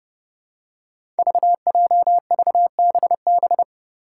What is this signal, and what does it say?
Morse code, heard as a keyed tone, about 28 wpm, 700 Hz. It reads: VJVBB
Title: Five Challenging Letter-Number Combinations 30wpm